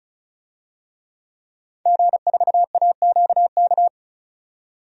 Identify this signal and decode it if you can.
Morse code, heard as a keyed tone, about 36 wpm, 700 Hz. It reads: G4AQK
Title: DXCC-100 Call Signs - 5 Characters - 35wpm